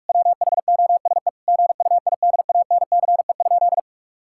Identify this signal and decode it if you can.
Morse code, heard as a keyed tone, about 41 wpm, 700 Hz. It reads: WHOSE GUIDANCE?
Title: Sentences from Top 1600 Words 45wpm